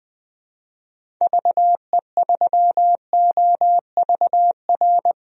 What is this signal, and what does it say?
Morse code, heard as a keyed tone, about 20 wpm, 700 Hz. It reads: VE3OVR
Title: DXCC-100 Call Signs - 6 Characters - 20wpm